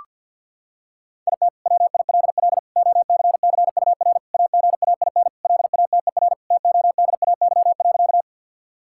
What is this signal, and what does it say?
Morse code, heard as a keyed tone, about 52 wpm, 700 Hz. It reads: IT WILL OCCUR AGAIN LATER TODAY.
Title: Sentences from Top 1000 Words - Review Part 6 - 50wpm